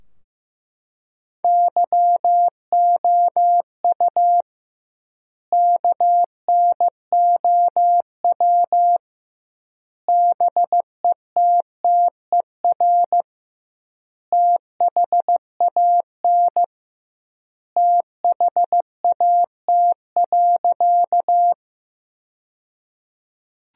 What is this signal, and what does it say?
Morse code, heard as a keyed tone, about 15 wpm, 700 Hz. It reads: YOU KNOW BETTER THAN THAT.
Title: 2x Word-Spacing - Sentences from Top 400 Words - 15wpm